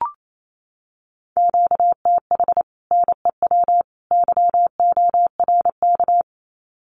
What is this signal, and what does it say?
Morse code, heard as a keyed tone, about 28 wpm, 700 Hz. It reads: QTH NEW YORK
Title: Common QSO Phrases 28wpm